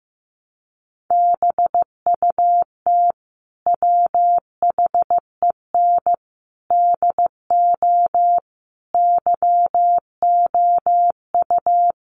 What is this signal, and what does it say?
Morse code, heard as a keyed tone, about 15 wpm, 700 Hz. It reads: BUT WHEN DO YOU
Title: Sentences from Top 1100 Words 15wpm